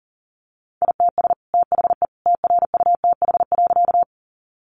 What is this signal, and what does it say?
Morse code, heard as a keyed tone, about 40 wpm, 700 Hz. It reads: ITS THE TRUTH.
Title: Sentences from Top 800 Words 40wpm